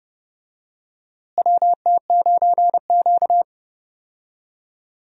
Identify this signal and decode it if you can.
Morse code, heard as a keyed tone, about 30 wpm, 700 Hz. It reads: WT9Q
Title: USA Call Signs - 4 Characters - Repeated 3x - 30wpm